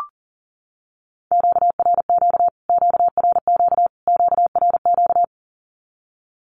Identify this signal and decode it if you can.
Morse code, heard as a keyed tone, about 40 wpm, 700 Hz. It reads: QRQ QRQ QRQ
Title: Q-Codes Repeated 3x 40wpm